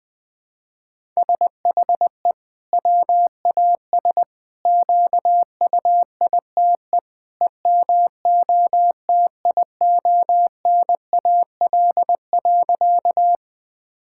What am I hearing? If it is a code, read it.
Morse code, heard as a keyed tone, about 21 wpm, 700 Hz. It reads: SHE WAS QUITE EMOTIONAL.
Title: Sentences from Top 1000 Words - Review Part 5 - 20wpm